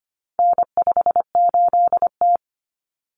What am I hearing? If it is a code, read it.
Morse code, heard as a keyed tone, about 25 wpm, 700 Hz. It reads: N58T
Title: Common Cut Numbers - 4-Digit 25wpm